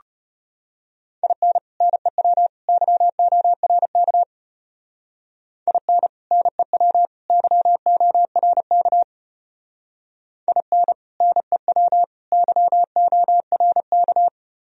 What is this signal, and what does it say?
Morse code, heard as a keyed tone, about 33 wpm, 700 Hz. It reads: IN NEW YORK IN NEW YORK IN NEW YORK